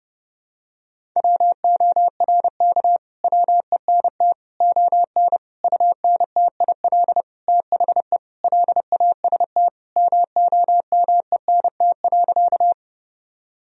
Morse code, heard as a keyed tone, about 32 wpm, 700 Hz. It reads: WORK WENT ON UNTIL THE LAST MOMENT.
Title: Sentences from Top 1000 Words - Review Part 1 - 30wpm